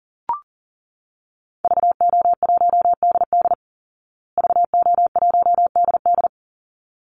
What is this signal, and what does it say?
Morse code, heard as a keyed tone, about 40 wpm, 700 Hz. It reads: VO1DD VO1DD
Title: Call Signs Repeated 3x - 40wpm